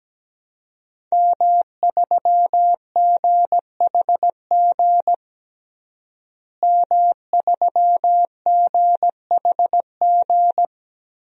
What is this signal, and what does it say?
Morse code, heard as a keyed tone, about 17 wpm, 700 Hz. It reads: M3GHG M3GHG
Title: DXCC-100 Call Signs - 5 Characters - Repeated 2x - 17wpm